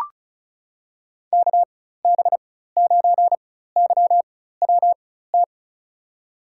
Morse code, heard as a keyed tone, about 36 wpm, spaced as for 16 wpm, 700 Hz. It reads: KB9YWT
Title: USA Call Signs - 6 Characters - 4x Character Spacing - 35wpm